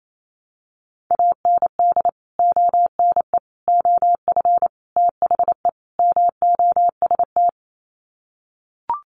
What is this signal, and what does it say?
Morse code, heard as a keyed tone, about 28 wpm, 700 Hz. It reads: AND ONE OF THE MOST
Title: Sets of 5 Words Encore 28wpm